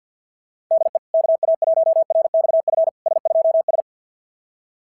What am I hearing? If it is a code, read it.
Morse code, heard as a keyed tone, about 50 wpm, 660 Hz. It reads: DE KA1RXF S2S
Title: SOTA QSO Practice 50wpm